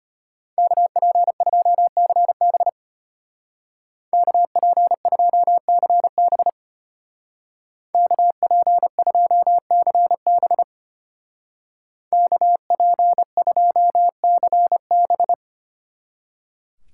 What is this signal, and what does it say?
Morse code, heard as a keyed tone, about 30 wpm, 700 Hz. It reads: KP2CB KP2CB KP2CB KP2CB
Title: Speed Racing - USA 2x2 Call Signs 25wpm